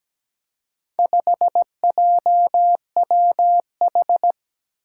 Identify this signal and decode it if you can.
Morse code, heard as a keyed tone, about 17 wpm, 700 Hz. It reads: 5JWH
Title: Four Challenging Letter-Number Combinations 17wpm